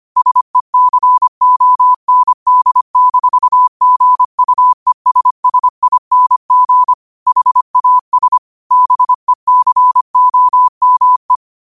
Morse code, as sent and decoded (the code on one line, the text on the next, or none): ... -.-. --- -. -.. -....- --. ..- . ... ... .. -. --. / .... .- ... / -... -.-. --- -- .
SECOND-GUESSING HAS BECOME